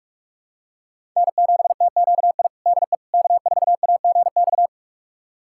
Morse code, heard as a keyed tone, about 47 wpm, 700 Hz. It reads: N7TQI DE K4AGX
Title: USA Callsign DE Callsign 45wpm